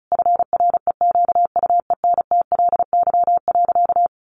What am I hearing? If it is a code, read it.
Morse code, heard as a keyed tone, about 34 wpm, 700 Hz. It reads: FREQUENTLY.
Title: Sentences from Top 1300 Words 35wpm